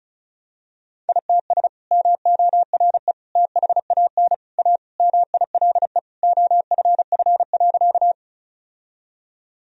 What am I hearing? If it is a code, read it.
Morse code, heard as a keyed tone, about 34 wpm, 700 Hz. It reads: ITS MORE THAN A MILE OFF.
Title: Sentences from Top 1000 Words - Review Part 1 - 35wpm